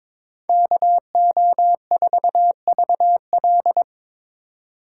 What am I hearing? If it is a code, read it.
Morse code, heard as a keyed tone, about 26 wpm, 700 Hz. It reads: KO4VL
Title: Speed Racing - USA 2x2 Call Signs 22wpm